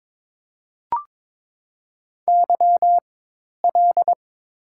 Morse code, heard as a keyed tone, about 21 wpm, spaced as for 10 wpm, 700 Hz. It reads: YL
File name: Top 90 QSO Elements - 4x Character Spacing - 22wpm